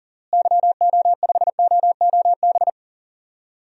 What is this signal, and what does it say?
Morse code, heard as a keyed tone, about 37 wpm, 700 Hz. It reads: YO5OOB
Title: DXCC-100 Call Signs - 6 Characters - Repeated 3x - 40wpm